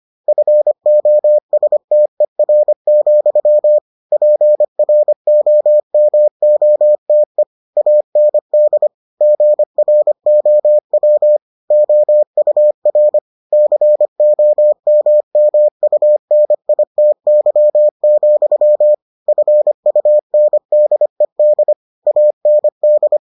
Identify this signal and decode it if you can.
Morse code, heard as a keyed tone, about 25 wpm, 600 Hz. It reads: FOSTER, PROMOTE AND GROW OUR COMMUNITY, FUNDED AND